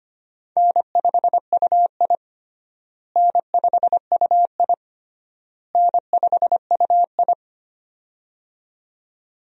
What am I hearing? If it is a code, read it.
Morse code, heard as a keyed tone, about 25 wpm, 700 Hz. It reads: N5UI N5UI N5UI
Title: USA Call Signs - 4 Characters - Repeated 3x - 25wpm